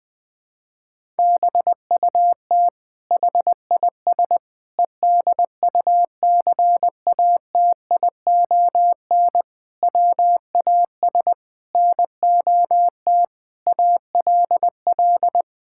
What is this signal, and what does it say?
Morse code, heard as a keyed tone, about 20 wpm, 700 Hz. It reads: BUT HIS EDUCATION WAS NOT ALL
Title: Sentences from Top 1000 Words - Review Part 6 - 20wpm